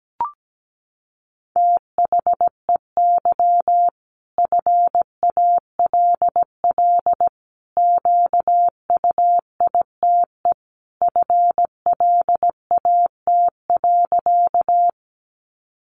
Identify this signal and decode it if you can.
Morse code, heard as a keyed tone, about 17 wpm, 700 Hz. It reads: THEY FALL QUITE FLAT.
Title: Sentences from Top 1000 Words 17wpm